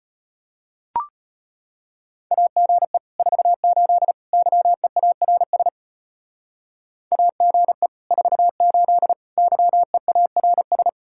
Morse code, heard as a keyed tone, about 32 wpm, 700 Hz. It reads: AGE 48 YEARS AGE 48 YEARS